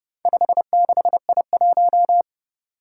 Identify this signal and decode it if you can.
Morse code, heard as a keyed tone, about 30 wpm, 700 Hz. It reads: 56I1
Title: Four Challenging Letter-Number Combinations 30wpm